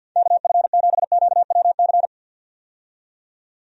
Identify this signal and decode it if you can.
Morse code, heard as a keyed tone, about 53 wpm, 700 Hz. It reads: KF7QWC
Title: USA Call Signs - 6 Characters - 50wpm